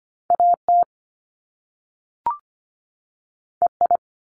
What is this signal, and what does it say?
Morse code, heard as a keyed tone, about 25 wpm, 700 Hz. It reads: AT EI
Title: Top Two Letter Combinations in Words 25wpm